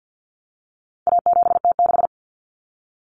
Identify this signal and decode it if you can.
Morse code, heard as a keyed tone, about 50 wpm, 700 Hz. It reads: A7T6
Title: Common Cut Numbers - 4-Digit 50wpm